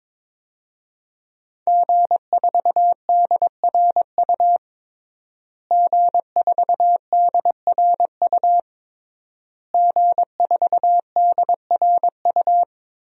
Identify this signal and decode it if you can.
Morse code, heard as a keyed tone, about 22 wpm, 700 Hz. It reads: G4DRU G4DRU G4DRU
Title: DXCC-100 Call Signs - 5 Characters - Repeated 3x - 22wpm